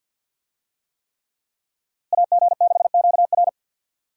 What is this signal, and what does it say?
Morse code, heard as a keyed tone, about 52 wpm, 700 Hz. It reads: AG6QR
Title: USA Call Signs - 5 Characters - Repeated 3x - 50wpm